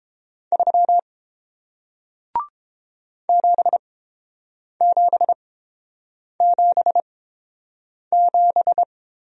Morse code, heard as a keyed tone, about 30 wpm, spaced as for 7 wpm, 700 Hz. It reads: 3 7777